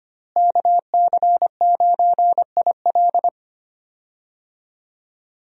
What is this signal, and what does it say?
Morse code, heard as a keyed tone, about 25 wpm, 700 Hz. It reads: KC9IL
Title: USA Call Signs - 5 Characters - Repeated 2x - 25wpm